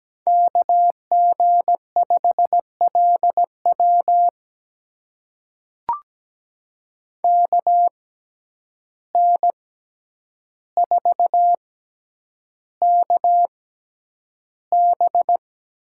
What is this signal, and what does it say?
Morse code, heard as a keyed tone, about 17 wpm, spaced as for 9 wpm, 700 Hz. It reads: KG5LW KN4KB
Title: USA Call Signs - 6x Character Spacing - 17wpm